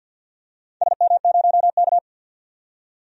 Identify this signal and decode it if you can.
Morse code, heard as a keyed tone, about 48 wpm, 700 Hz. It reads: SM0K
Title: DXCC-100 Call Signs - 4 Characters - 50wpm